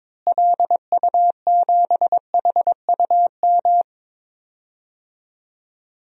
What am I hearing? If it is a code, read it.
Morse code, heard as a keyed tone, about 22 wpm, 700 Hz. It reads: LU7HUM
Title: DXCC-100 Call Signs - 6 Characters - Repeated 2x - 22wpm